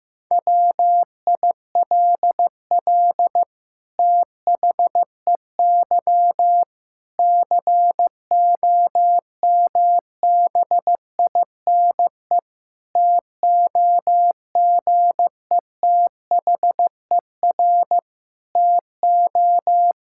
Morse code, heard as a keyed tone, about 15 wpm, 700 Hz. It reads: WILL THEY COMBINE TOGETHER TO
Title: Sentences from Top 1600 Words 15wpm